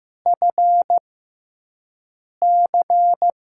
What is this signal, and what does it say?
Morse code, heard as a keyed tone, about 15 wpm, spaced as for 5 wpm, 700 Hz. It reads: FC
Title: USA Call Signs - 5 Characters - 6x Character Spacing - 15wpm